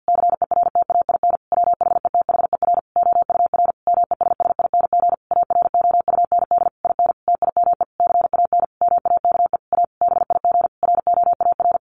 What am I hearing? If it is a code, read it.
Morse code, heard as a keyed tone, about 50 wpm, 700 Hz. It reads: CERTAIN WHETHER OUR MESSING AROUND IN TIME CAN MAKE A BIG ROAR
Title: A Sound Of Thunder 50wpm